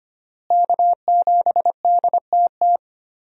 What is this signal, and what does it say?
Morse code, heard as a keyed tone, about 25 wpm, 700 Hz. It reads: K7DTT
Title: USA Call Signs - 5 Characters - Repeated 3x - 25wpm